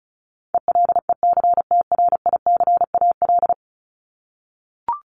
Morse code, heard as a keyed tone, about 34 wpm, 700 Hz. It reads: ELECTRICAL
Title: Top 751-1000 Words 35wpm